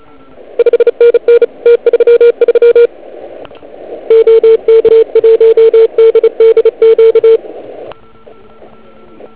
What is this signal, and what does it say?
Morse code, heard as a keyed tone, about 38 wpm, 470 Hz. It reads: 5NN T33 OK1DDQ